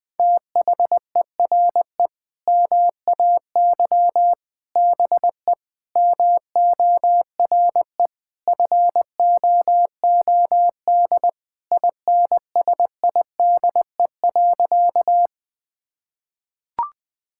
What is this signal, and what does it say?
Morse code, heard as a keyed tone, about 20 wpm, 700 Hz. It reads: THERE MAY BE MORE FOOD INSIDE.
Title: Sentences from Top 1000 Words - Review Part 6 - 20wpm